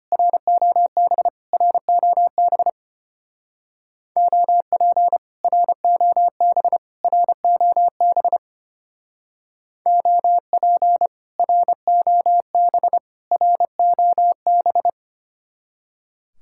Morse code, heard as a keyed tone, about 30 wpm, 700 Hz. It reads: ROB ROB OP ROB ROB OP ROB ROB
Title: Speed-Racing Common QSO Phrases - 25wpm